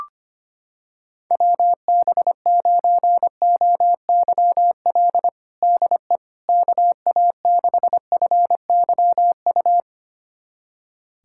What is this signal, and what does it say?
Morse code, heard as a keyed tone, about 25 wpm, 700 Hz. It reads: WB9OYL DE KA6FYU
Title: USA Callsign DE Callsign - 6 Characters - 25wpm